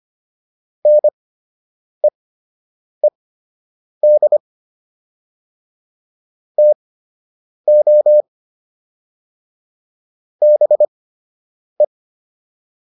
Morse code, heard as a keyed tone, about 25 wpm, spaced as for 8 wpm, 600 Hz. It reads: NEED TO BE